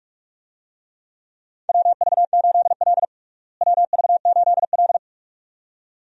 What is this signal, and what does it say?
Morse code, heard as a keyed tone, about 42 wpm, 700 Hz. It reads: WV8L WV8L
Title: USA Call Signs - 4 Characters - Repeated 2x - 45wpm